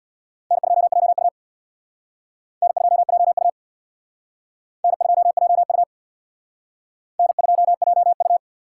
Morse code, heard as a keyed tone, about 61 wpm, 700 Hz. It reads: N2JU N2JU N2JU N2JU
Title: Speed Racing - USA 1x2 Call Signs 50wpm